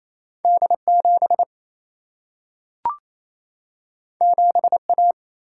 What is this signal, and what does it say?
Morse code, heard as a keyed tone, about 28 wpm, 700 Hz. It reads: D7 7A
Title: Single-Letter Single-Number 28wpm